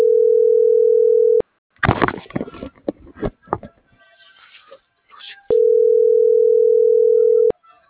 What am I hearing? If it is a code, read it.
Ringback tone (North America), 440 + 480 Hz.